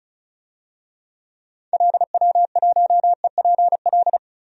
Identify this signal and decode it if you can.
Morse code, heard as a keyed tone, about 34 wpm, 700 Hz. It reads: LW1EPL
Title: DXCC-100 Call Signs - 6 Characters - 35wpm